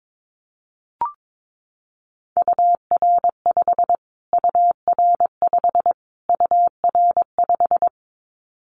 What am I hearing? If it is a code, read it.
Morse code, heard as a keyed tone, about 22 wpm, 700 Hz. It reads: UR5 UR5 UR5
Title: Callsign Prefixes for Top 25 DXCC - Repeated 3x - 22wpm